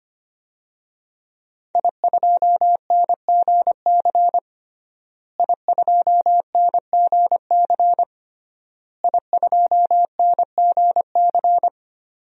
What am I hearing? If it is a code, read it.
Morse code, heard as a keyed tone, about 25 wpm, 700 Hz. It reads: I2NGC I2NGC I2NGC
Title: DXCC-100 Call Signs - 5 Characters - Repeated 3x - 25wpm